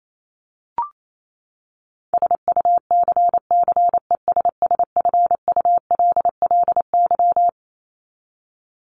Morse code, heard as a keyed tone, about 28 wpm, 700 Hz. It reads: SUCCESSFULLY
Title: Top 1501-1750 Words 28wpm